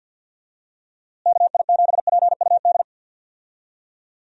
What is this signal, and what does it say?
Morse code, heard as a keyed tone, about 51 wpm, 700 Hz. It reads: KI7PUD
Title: USA Call Signs - 6 Characters - Repeated 2x - 50wpm